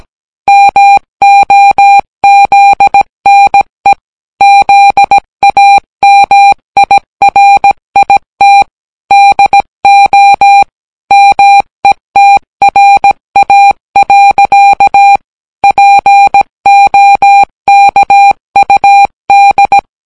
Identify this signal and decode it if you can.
Morse code, heard as a keyed tone, about 17 wpm, 800 Hz. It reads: MOZNE ZAMIRIT DO METRA. POKUD